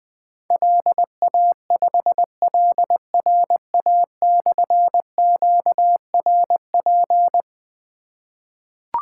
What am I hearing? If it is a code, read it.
Morse code, heard as a keyed tone, about 20 wpm, 700 Hz. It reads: LA5LRA/QRP
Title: Difficult Call Signs 20wpm